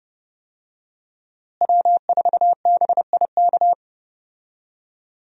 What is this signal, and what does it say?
Morse code, heard as a keyed tone, about 32 wpm, 700 Hz. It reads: W4BIK
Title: USA Call Signs - 5 Characters - Repeated 2x - 30wpm